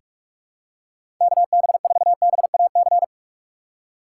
Morse code, heard as a keyed tone, about 46 wpm, 690 Hz. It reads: KB4BAC
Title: USA Call Signs - 6 Characters - 3x Character Spacing - 45wpm